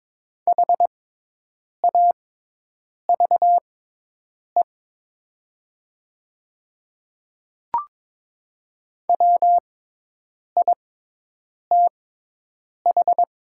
Morse code, heard as a keyed tone, about 22 wpm, spaced as for 8 wpm, 700 Hz. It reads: HAVE WITH
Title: 6x Character Spacing - Top 100 Words - 22wpm